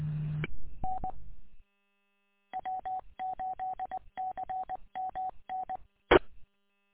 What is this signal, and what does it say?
Morse code, heard as a keyed tone, about 20 wpm, 750 Hz. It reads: N W8CMN E
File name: w8cmn- -combined